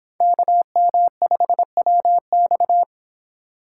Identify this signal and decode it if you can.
Morse code, heard as a keyed tone, about 30 wpm, 700 Hz. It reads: KM5WX